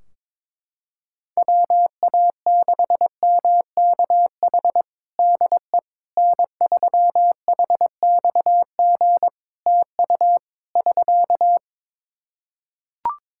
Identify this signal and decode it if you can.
Morse code, heard as a keyed tone, about 22 wpm, 700 Hz. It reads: WA6MKH DE N3HXG TU <SK>
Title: Practice QSO - Exchange 4 - Ending - 22wpm